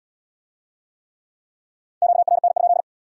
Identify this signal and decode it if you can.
Morse code, heard as a keyed tone, about 75 wpm, 700 Hz. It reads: QRN?